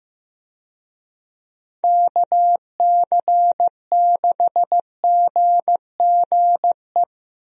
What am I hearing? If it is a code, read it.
Morse code, heard as a keyed tone, about 15 wpm, 700 Hz. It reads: KC6GGE
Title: USA Call Signs - 6 Characters - Repeated 3x - 15wpm